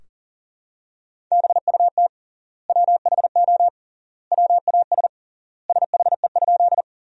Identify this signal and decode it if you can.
Morse code, heard as a keyed tone, about 40 wpm, 700 Hz. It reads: BUT WHO WAS SHE?
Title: 3x Word-Spacing - Sentences from Top 100 Words - 40wpm